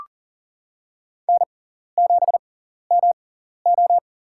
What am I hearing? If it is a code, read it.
Morse code, heard as a keyed tone, about 37 wpm, spaced as for 14 wpm, 700 Hz. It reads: N7MO
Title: USA Call Signs - 4 Characters - 6x Character Spacing - 40wpm